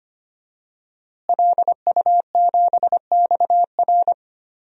Morse code, heard as a keyed tone, about 25 wpm, 700 Hz. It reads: LU7XR
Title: DXCC-100 Call Signs - 5 Characters - Repeated 3x - 25wpm